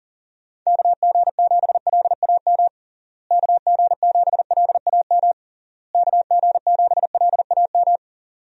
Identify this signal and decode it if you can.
Morse code, heard as a keyed tone, about 40 wpm, 700 Hz. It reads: KG7LAM KG7LAM KG7LAM
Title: USA Call Signs - 6 Characters - Repeated 3x - 40wpm